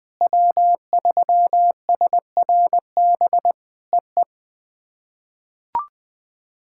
Morse code, heard as a keyed tone, about 20 wpm, 700 Hz. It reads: W3SRB EE
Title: POTA QSO Practice - 20wpm